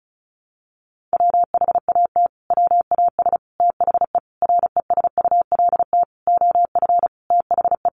Morse code, heard as a keyed tone, about 34 wpm, 700 Hz. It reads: WHAT WAS THE RESULT OF THE
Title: Sentences from Top 1500 Words 35wpm